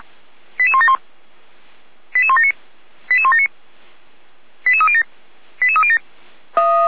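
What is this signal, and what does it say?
Five-tone selective call (ZVEI), heard as quick runs of five tones, tones of about 70 ms. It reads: 88171 88179 88179 88307 88307
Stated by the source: ZVEI five-tone call 88352 received on an SDR.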